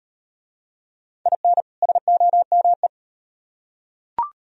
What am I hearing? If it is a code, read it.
Morse code, heard as a keyed tone, about 38 wpm, 700 Hz. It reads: IN SOME